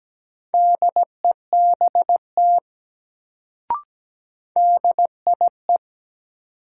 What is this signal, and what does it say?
Morse code, heard as a keyed tone, about 17 wpm, 700 Hz. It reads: DEBT DIE
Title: Sentences from Top 1300 Words - New Word Drill - 17wpm